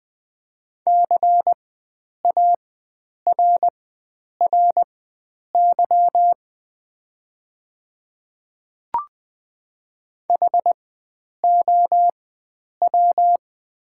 Morse code, heard as a keyed tone, about 20 wpm, spaced as for 9 wpm, 690 Hz. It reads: CARRY HOW
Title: 4x Character Spacing - Top 300 Words - 20wpm